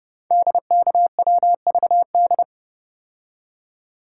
Morse code, heard as a keyed tone, about 30 wpm, 700 Hz. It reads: DKWVD
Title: Code Groups - 5 Letters - 30wpm